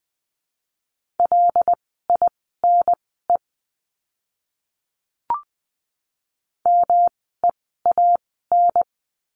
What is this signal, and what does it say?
Morse code, heard as a keyed tone, about 20 wpm, spaced as for 14 wpm, 700 Hz. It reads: LINE MEAN
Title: 115 Lesson 19f - Letter w - Words 20wpm